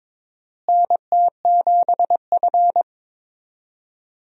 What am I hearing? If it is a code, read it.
Morse code, heard as a keyed tone, about 22 wpm, 700 Hz. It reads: NT7F
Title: USA Call Signs - 4 Characters - Repeated 3x - 22wpm